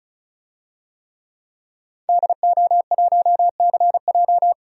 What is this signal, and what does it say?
Morse code, heard as a keyed tone, about 36 wpm, 700 Hz. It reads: DO1CJ
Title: DXCC-100 Call Signs - 5 Characters - 35wpm